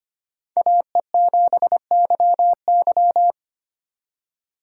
Morse code, heard as a keyed tone, about 25 wpm, 700 Hz. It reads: AE7YY
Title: USA Call Signs - 5 Characters - 2x Character Spacing - 25wpm